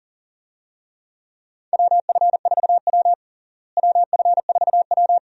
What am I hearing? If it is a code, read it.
Morse code, heard as a keyed tone, about 40 wpm, 700 Hz. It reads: WF4W WF4W
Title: USA Call Signs - 4 Characters - Repeated 2x - 40wpm